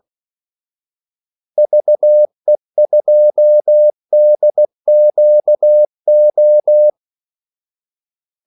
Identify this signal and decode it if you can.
Morse code, heard as a keyed tone, about 16 wpm, 600 Hz. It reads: VE2DQO
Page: Kaldesignaler 16 wpm